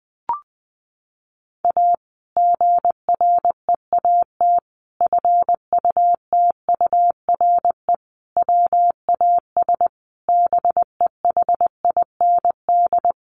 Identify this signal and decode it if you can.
Morse code, heard as a keyed tone, about 20 wpm, 700 Hz. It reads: A GREAT FUTURE WAS BEHIND
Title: Sentences from Top 700 Words 20wpm